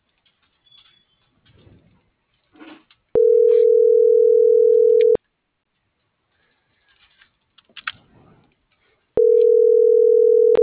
Ringback tone (North America), 440 + 480 Hz.